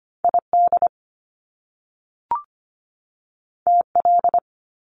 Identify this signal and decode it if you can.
Morse code, heard as a keyed tone, about 25 wpm, 700 Hz. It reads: ID TL
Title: Top Two Letter Combinations in Words 25wpm